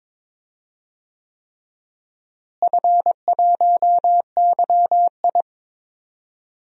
Morse code, heard as a keyed tone, about 22 wpm, 700 Hz. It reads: F1YI
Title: DXCC-100 Call Signs - 4 Characters - Repeated 3x - 22wpm